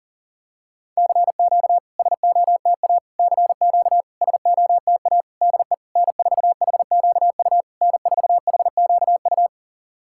Morse code, heard as a keyed tone, about 40 wpm, 690 Hz. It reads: CQ SOTA CQ SOTA DE N4HQU N4HQU
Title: SOTA QSO Practice 40wpm